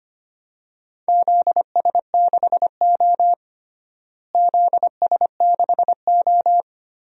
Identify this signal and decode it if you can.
Morse code, heard as a keyed tone, about 25 wpm, 700 Hz. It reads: ZS6O ZS6O
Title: DXCC-100 Call Signs - 4 Characters - Repeated 3x - 25wpm